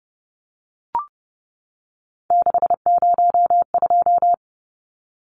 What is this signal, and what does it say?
Morse code, heard as a keyed tone, about 30 wpm, 700 Hz. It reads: 602
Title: Numbers - 3-Digit 30wpm